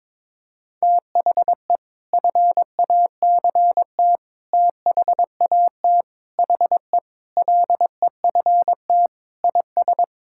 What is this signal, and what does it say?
Morse code, heard as a keyed tone, about 21 wpm, 700 Hz. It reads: THE FACT THAT HE LEFT IS
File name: practice / 22wpm / Sentences with 2-Word Phrases - 22wpm